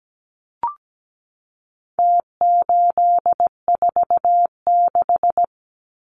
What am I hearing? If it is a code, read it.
Morse code, heard as a keyed tone, about 17 wpm, 700 Hz. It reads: T846
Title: Numbers - Contest Style - 4-Digit 17wpm